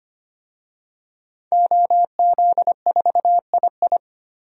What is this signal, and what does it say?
Morse code, heard as a keyed tone, about 25 wpm, 700 Hz. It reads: OZ4II
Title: DXCC-100 Call Signs - 5 Characters - 25wpm